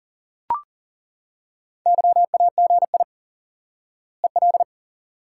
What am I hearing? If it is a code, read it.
Morse code, heard as a keyed tone, about 38 wpm, 700 Hz. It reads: YAGI EL
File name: IQE Course 3 - Lesson 89 - YAGI - Two Elements - 40wpm